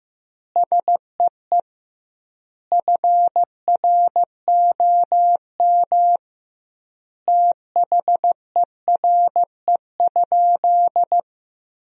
Morse code, heard as a keyed tone, about 15 wpm, 700 Hz. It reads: SEE FROM THERE?
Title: 2x Word-Spacing - Sentences from Top 100 Words - 15wpm